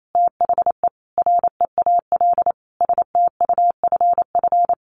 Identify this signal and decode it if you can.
Morse code, heard as a keyed tone, about 28 wpm, 700 Hz. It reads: THE REAL STUFF
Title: Sentences from Top 1400 Words 28wpm